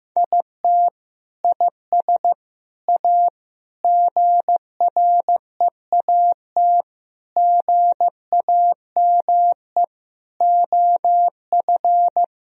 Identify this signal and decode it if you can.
Morse code, heard as a keyed tone, about 15 wpm, 700 Hz. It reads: IT IS A GREAT GAME OF
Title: Sentences from Top 1000 Words - Review Part 3 - 15wpm